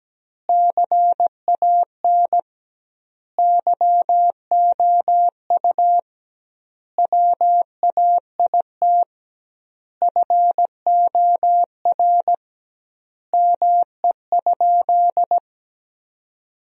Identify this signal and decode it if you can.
Morse code, heard as a keyed tone, about 17 wpm, 700 Hz. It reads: CAN YOU WAIT FOR ME?
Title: 2x Word-Spacing - Sentences from Top 500 Words - 17wpm